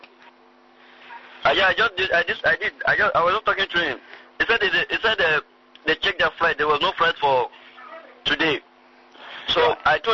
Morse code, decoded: none